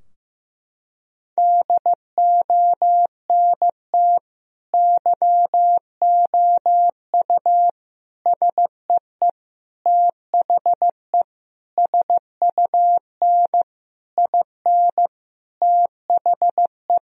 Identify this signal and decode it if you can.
Morse code, heard as a keyed tone, about 16 wpm, 700 Hz. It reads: DONT YOU SEE THE SUN IN THE
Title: Sentences from Top 1000 Words - Review Part 2 - 15wpm